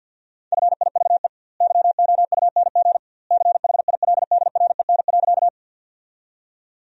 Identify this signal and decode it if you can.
Morse code, heard as a keyed tone, about 47 wpm, 700 Hz. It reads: FIVE YOUNG CHILDREN.
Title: 2x Word-Spacing - Sentences from Top 400 Words - 50wpm